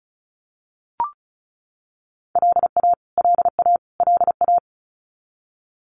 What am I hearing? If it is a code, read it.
Morse code, heard as a keyed tone, about 34 wpm, 700 Hz. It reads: LA LA LA
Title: US States Repeated 3x 35wpm